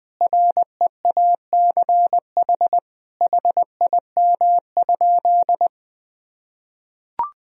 Morse code, heard as a keyed tone, about 20 wpm, 700 Hz. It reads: REACH HIM?
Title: Sentences from Top 1000 Words - Review Part 2 - 20wpm